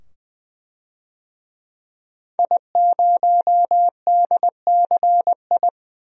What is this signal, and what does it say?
Morse code, heard as a keyed tone, about 20 wpm, 700 Hz. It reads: I0DCI
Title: DXCC-100 Call Signs - 5 Characters - 20wpm